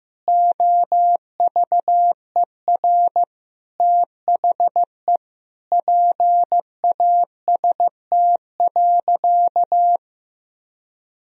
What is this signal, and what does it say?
Morse code, heard as a keyed tone, about 16 wpm, 700 Hz. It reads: OVER THE PAST.
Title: Sentences from Top 1000 Words - Review Part 3 - 15wpm